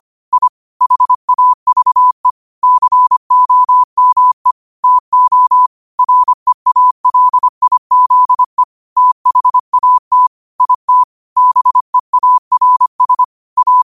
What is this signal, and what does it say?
Morse code, heard as a keyed tone, about 25 wpm, 1000 Hz. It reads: I HAVE COME TO REALIZE THAT IT BEARS A